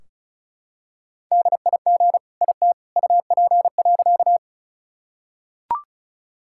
Morse code, heard as a keyed tone, about 36 wpm, 700 Hz. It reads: DIG IT UP.
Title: Sentences from Top 1600 Words 35wpm